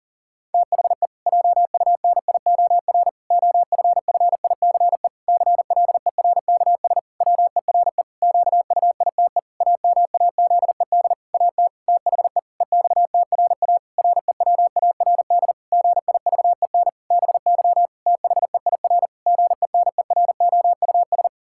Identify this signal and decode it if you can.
Morse code, heard as a keyed tone, about 40 wpm, 700 Hz. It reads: THE JUNIOR OFFICE CLERKS WERE QUITE AMAZED AT THE EXTRA REWARD GIVEN BY THEIR GENEROUS
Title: Pangram Challenge v2 - 40wpm